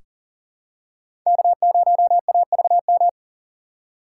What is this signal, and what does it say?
Morse code, heard as a keyed tone, about 40 wpm, 700 Hz. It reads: K0AVM
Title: USA Call Signs - 5 Characters - 3x Character Spacing - 40wpm